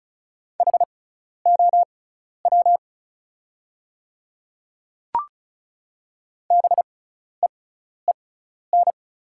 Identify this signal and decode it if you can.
Morse code, heard as a keyed tone, about 36 wpm, spaced as for 12 wpm, 700 Hz. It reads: HOW BEEN
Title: 6x Character Spacing - Top 100 Words - 35wpm